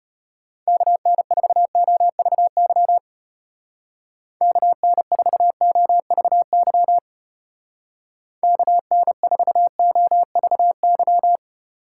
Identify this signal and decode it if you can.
Morse code, heard as a keyed tone, about 32 wpm, 700 Hz. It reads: KN4OVY KN4OVY KN4OVY